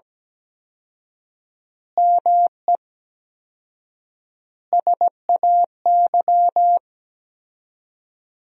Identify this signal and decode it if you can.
Morse code, heard as a keyed tone, about 17 wpm, 700 Hz. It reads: ME SAY
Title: 4x Word-Spacing - Sentences from Top 500 Words - 17wpm